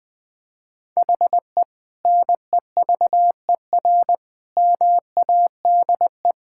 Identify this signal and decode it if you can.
Morse code, heard as a keyed tone, about 21 wpm, 700 Hz. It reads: HE NEVER MADE
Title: Sentences from Top 1600 Words 20wpm